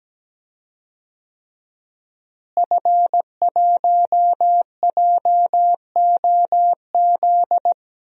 Morse code, heard as a keyed tone, about 17 wpm, 700 Hz. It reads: F1JOZ
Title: DXCC-100 Call Signs - 5 Characters - Repeated 3x - 17wpm